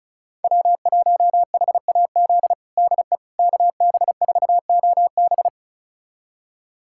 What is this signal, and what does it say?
Morse code, heard as a keyed tone, about 36 wpm, 700 Hz. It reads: W1HAZ DE KB4OB
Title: USA Callsign DE Callsign - 5 Characters - 35wpm